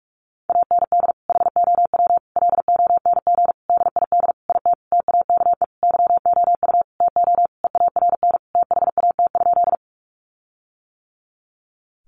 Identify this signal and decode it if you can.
Morse code, heard as a keyed tone, about 44 wpm, 700 Hz. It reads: AND HOW LONG DID IT TAKE YOU TO EARN THAT?
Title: Sentences from Top 1500 Words 45wpm